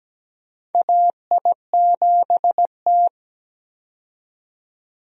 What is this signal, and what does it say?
Morse code, heard as a keyed tone, about 17 wpm, 700 Hz. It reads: AI7T
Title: USA Call Signs - 4 Characters - 17wpm